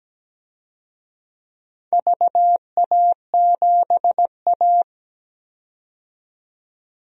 Morse code, heard as a keyed tone, about 17 wpm, 700 Hz. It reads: VA7A
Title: DXCC-100 Call Signs - 4 Characters - 17wpm